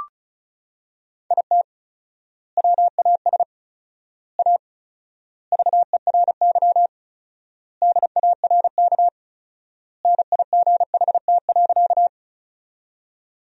Morse code, heard as a keyed tone, about 36 wpm, 700 Hz. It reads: IT WAS A VERY DARK NIGHT.
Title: 4x Word-Spacing - Sentences from Top 500 Words - 35wpm